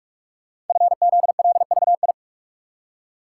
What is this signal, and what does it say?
Morse code, heard as a keyed tone, about 44 wpm, 700 Hz. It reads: FZLVI
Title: Code Groups - 5 Letters - 45wpm